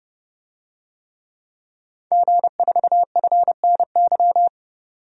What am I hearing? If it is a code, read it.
Morse code, heard as a keyed tone, about 30 wpm, 700 Hz. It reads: G4FNY
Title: DXCC-100 Call Signs - 5 Characters - 30wpm